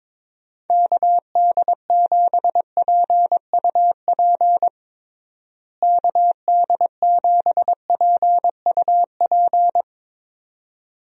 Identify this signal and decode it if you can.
Morse code, heard as a keyed tone, about 22 wpm, 700 Hz. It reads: KD7PUP KD7PUP
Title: USA Call Signs - 6 Characters - Repeated 2x - 22wpm